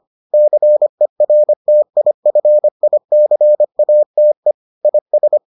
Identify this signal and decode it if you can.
Morse code, heard as a keyed tone, about 25 wpm, 600 Hz. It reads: CERTIFICATE IS